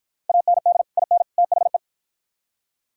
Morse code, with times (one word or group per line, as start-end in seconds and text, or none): 0.3-0.8 AND
1.0-1.2 IN
1.4-1.8 THE